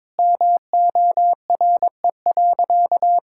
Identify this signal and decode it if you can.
Morse code, heard as a keyed tone, about 22 wpm, 700 Hz. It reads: MORE.
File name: practice / 22wpm / Sentences with 3-Word Phrases - 22wpm